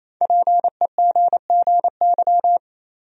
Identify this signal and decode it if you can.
Morse code, heard as a keyed tone, about 28 wpm, 700 Hz. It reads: PEGGY
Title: USA Common Names - 28wpm